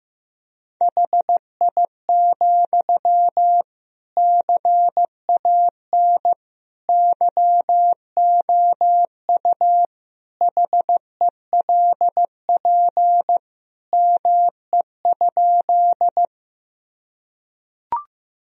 Morse code, heard as a keyed tone, about 16 wpm, 700 Hz. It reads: HI, CAN YOU HELP ME?
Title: Sentences from Top 1000 Words - Review Part 3 - 15wpm